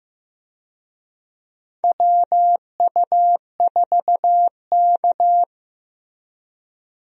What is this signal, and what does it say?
Morse code, heard as a keyed tone, about 15 wpm, 700 Hz. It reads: WU4K
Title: USA Call Signs - 4 Characters - Repeated 3x - 15wpm